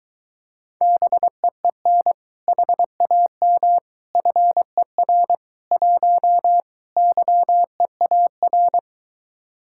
Morse code, heard as a keyed tone, about 23 wpm, 700 Hz. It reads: BEEN HAM FER 1 YEAR